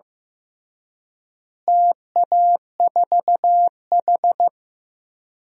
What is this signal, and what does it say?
Morse code, heard as a keyed tone, about 15 wpm, 700 Hz. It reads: TA4H
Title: DXCC-100 Call Signs - 4 Characters - Repeated 3x - 15wpm